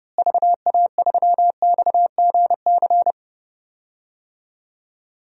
Morse code, heard as a keyed tone, about 30 wpm, 700 Hz. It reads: VA3XGC
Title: DXCC-100 Call Signs - 6 Characters - Repeated 3x - 30wpm